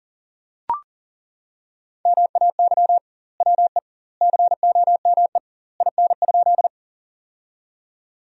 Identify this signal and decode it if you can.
Morse code, heard as a keyed tone, about 40 wpm, 700 Hz. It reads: MAY WE COME IN?
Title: 2x Word-Spacing - Sentences from Top 100 Words - 40wpm